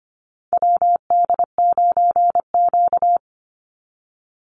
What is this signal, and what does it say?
Morse code, heard as a keyed tone, about 25 wpm, 700 Hz. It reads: WD9Q
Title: Call Signs - 25wpm